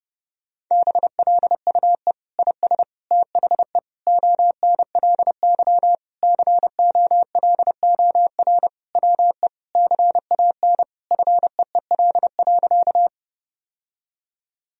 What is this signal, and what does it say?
Morse code, heard as a keyed tone, about 30 wpm, 700 Hz. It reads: BLUE IS THE ONLY COLOR WE CAN FEEL.
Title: Sentences from Top 1000 Words - Review Part 3 - 30wpm